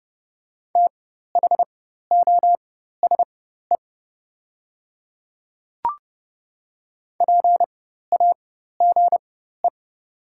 Morse code, heard as a keyed tone, about 30 wpm, spaced as for 14 wpm, 700 Hz. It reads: THOSE PAGE
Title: 4x Character Spacing - Top 300 Words - 30wpm